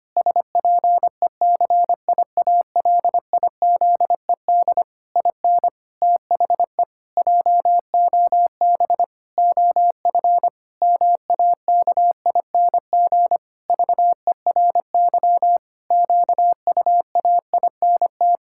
Morse code, heard as a keyed tone, about 25 wpm, 700 Hz. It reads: SPECIALIZED IN THE JOB OF MAKING VERY QUAINT
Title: Pangram Challenge v2 - 25wpm